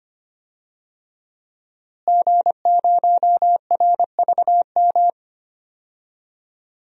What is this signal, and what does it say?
Morse code, heard as a keyed tone, about 25 wpm, 700 Hz. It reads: G0RVM
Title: DXCC-100 Call Signs - 5 Characters - Repeated 3x - 25wpm